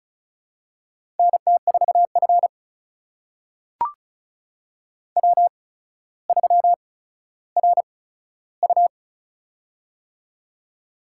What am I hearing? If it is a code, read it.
Morse code, heard as a keyed tone, about 36 wpm, spaced as for 15 wpm, 700 Hz. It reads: NT4F W3RU
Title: USA Call Signs - 4 Characters - 8x Character Spacing - 35wpm